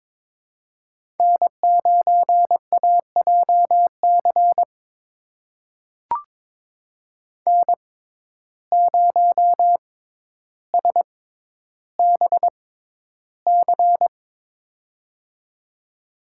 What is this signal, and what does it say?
Morse code, heard as a keyed tone, about 22 wpm, spaced as for 11 wpm, 700 Hz. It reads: N9AJC N0SBC
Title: USA Call Signs - 5 Characters - 6x Character Spacing - 22wpm